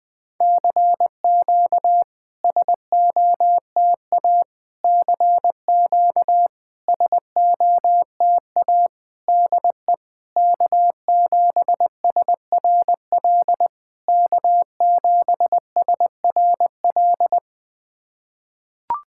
Morse code, heard as a keyed tone, about 18 wpm, 700 Hz. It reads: CQ SOTA CQ SOTA DE K7SRL K7SRL
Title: SOTA QSO Practice 20wpm